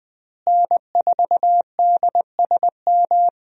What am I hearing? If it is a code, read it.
Morse code, heard as a keyed tone, about 20 wpm, 700 Hz. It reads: N4DSM
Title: USA Call Signs - 5 Characters - Repeated 3x - 20wpm